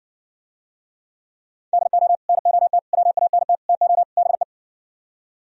Morse code, heard as a keyed tone, about 69 wpm, 700 Hz. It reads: DO NOT WANT TO BE